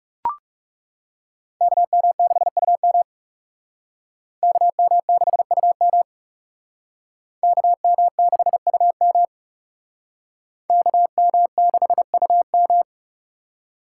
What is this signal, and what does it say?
Morse code, heard as a keyed tone, about 36 wpm, 700 Hz. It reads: KM6UM KM6UM KM6UM KM6UM
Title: Speed Racing - USA 2x2 Call Signs 30wpm